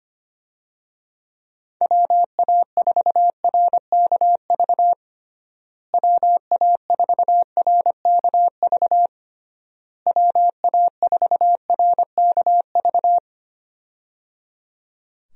Morse code, heard as a keyed tone, about 25 wpm, 700 Hz. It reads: WA4RKV WA4RKV WA4RKV
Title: USA Call Signs - 6 Characters - Repeated 3x - 25wpm